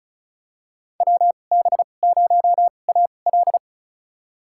Morse code, heard as a keyed tone, about 36 wpm, spaced as for 25 wpm, 700 Hz. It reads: WB0AL
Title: USA Call Signs - 5 Characters - 2x Character Spacing - 35wpm